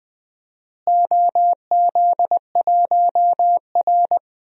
Morse code, heard as a keyed tone, about 21 wpm, 700 Hz. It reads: OZ1R
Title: DXCC-100 Call Signs - 4 Characters - Repeated 3x - 20wpm